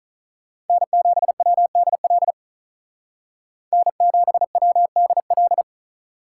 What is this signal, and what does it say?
Morse code, heard as a keyed tone, about 43 wpm, 700 Hz. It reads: N7WDL N7WDL
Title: Speed Racing - USA 1x3 Call Signs 35wpm